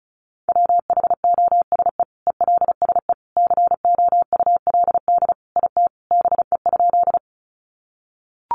Morse code, heard as a keyed tone, about 34 wpm, 700 Hz. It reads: WHOSE ELSE COULD IT BE?
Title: Sentences from Top 1200 Words 35wpm